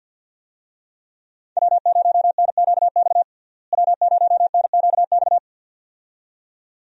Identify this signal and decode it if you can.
Morse code, heard as a keyed tone, about 47 wpm, 700 Hz. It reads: W0NQX W0NQX
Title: USA Call Signs - 5 Characters - Repeated 2x - 50wpm